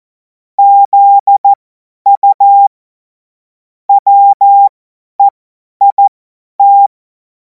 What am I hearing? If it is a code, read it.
Morse code, heard as a keyed tone, about 14 wpm, spaced as for 10 wpm, 800 Hz. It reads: ZU WEIT
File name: Morsekurs / Witze / Witz 03. März 2026 50 F70